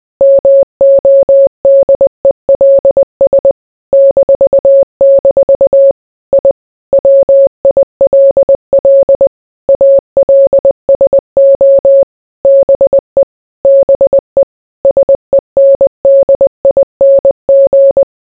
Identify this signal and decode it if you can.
Morse code, heard as a keyed tone, about 20 wpm, 550 Hz. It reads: MODELS -- I WILL ALSO BE BE SENDING